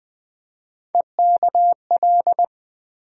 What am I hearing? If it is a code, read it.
Morse code, heard as a keyed tone, about 20 wpm, 700 Hz. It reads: EKL
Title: USA Call Sign Postfixes - 3 Letters 20wpm